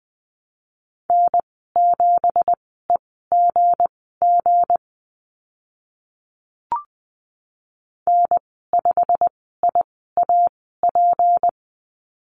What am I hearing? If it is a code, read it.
Morse code, heard as a keyed tone, about 20 wpm, spaced as for 14 wpm, 700 Hz. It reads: N7EGG N5IAP
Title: 173 Lesson 28g - Number 9 - Call Signs 20wpm